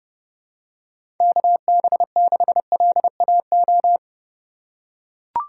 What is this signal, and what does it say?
Morse code, heard as a keyed tone, about 32 wpm, 700 Hz. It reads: KB6LAO
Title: USA Call Signs - 6 Characters - 6x Character Spacing - 30wpm